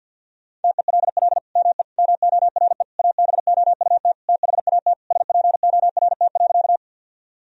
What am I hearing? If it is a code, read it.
Morse code, heard as a keyed tone, about 50 wpm, 700 Hz. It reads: TELL ME MORE ABOUT THAT SPORT.
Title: Sentences from Top 1000 Words - Review Part 4 - 50wpm